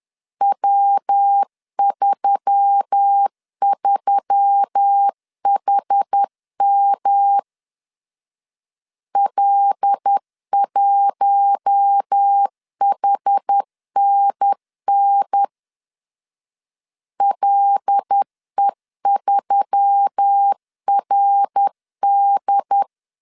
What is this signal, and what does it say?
Morse code, heard as a keyed tone, about 10 wpm, 800 Hz. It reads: W33HM L1HNN LE3RD